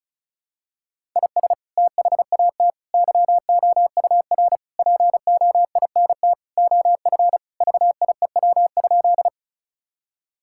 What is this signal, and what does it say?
Morse code, heard as a keyed tone, about 36 wpm, 700 Hz. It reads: IS THAT YOUR POINT OF VIEW?
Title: Sentences from Top 1000 Words - Review Part 6 - 35wpm